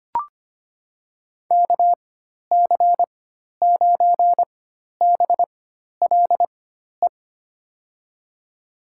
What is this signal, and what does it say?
Morse code, heard as a keyed tone, about 25 wpm, spaced as for 12 wpm, 700 Hz. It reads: KC9BLE
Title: USA Call Signs - 6 Characters - 4x Character Spacing - 25wpm